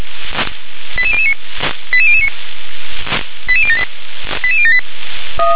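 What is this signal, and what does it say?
Five-tone selective call (ZVEI), heard as quick runs of five tones, tones of about 70 ms. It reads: 88009 88009 88078 88078
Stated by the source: ZVEI five-tone selective call received on SDR.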